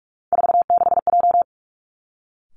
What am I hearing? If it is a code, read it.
Morse code, heard as a keyed tone, about 46 wpm, 700 Hz. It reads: 46J
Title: Three Challenging Letter-Number Combinations 45wpm